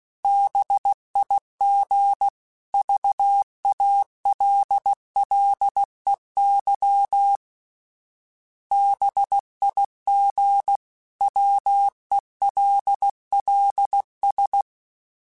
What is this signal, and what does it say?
Morse code, heard as a keyed tone, about 16 wpm, 800 Hz. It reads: BIG VALLEY BIG WELLS
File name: fromSMs / FromSMs mp3s / Cities@15